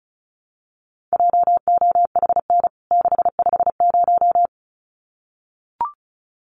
Morse code, heard as a keyed tone, about 34 wpm, 700 Hz. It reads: JOHN 650
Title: CWT Exchange 35wpm